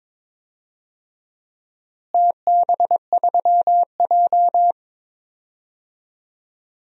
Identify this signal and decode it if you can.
Morse code, heard as a keyed tone, about 22 wpm, 700 Hz. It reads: TB3J
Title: DXCC-100 Call Signs - 4 Characters - Repeated 2x - 22wpm